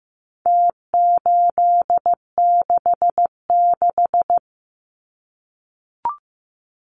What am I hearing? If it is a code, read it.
Morse code, heard as a keyed tone, about 14 wpm, 700 Hz. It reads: T866
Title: Numbers - Contest Style - 4-Digit 15wpm